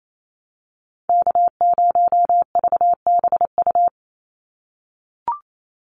Morse code, heard as a keyed tone, about 28 wpm, 700 Hz. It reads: K0VBU
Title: Call Signs - 28wpm